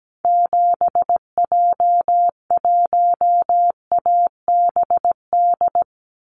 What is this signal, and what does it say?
Morse code, heard as a keyed tone, about 17 wpm, 700 Hz. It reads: 7J1ABD
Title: Call Signs - 17wpm